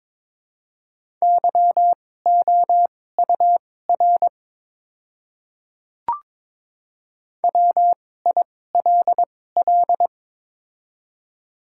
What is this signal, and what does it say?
Morse code, heard as a keyed tone, about 22 wpm, spaced as for 16 wpm, 700 Hz. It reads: YOUR WILL
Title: 2x Character Spacing - Top 200 Words - 22wpm